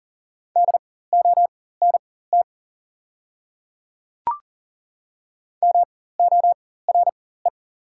Morse code, heard as a keyed tone, about 37 wpm, spaced as for 18 wpm, 690 Hz. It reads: DONT MORE
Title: 4x Character Spacing - Top 300 Words - 40wpm